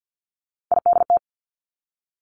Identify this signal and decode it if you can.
Morse code, heard as a keyed tone, about 50 wpm, 700 Hz. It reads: IDT
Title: Three Challenging Letter-Number Combinations 50wpm